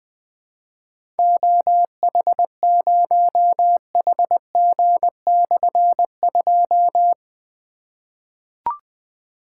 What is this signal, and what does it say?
Morse code, heard as a keyed tone, about 20 wpm, 700 Hz. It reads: OH0HG/2
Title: Difficult Call Signs 20wpm